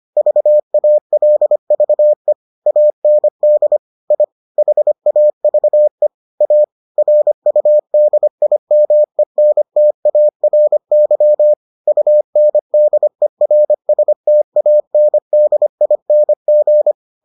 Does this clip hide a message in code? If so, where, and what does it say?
Yes: Morse code from 0.2 to 16.9 s, heard as a keyed tone, about 25 wpm, 600 Hz. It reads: VALVE AND I HAVE A RUDIMENTARY UNDERSTANDING